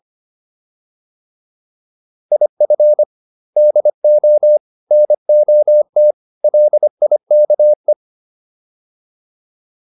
Morse code, heard as a keyed tone, about 25 wpm, 600 Hz. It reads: IF DO NOT LIKE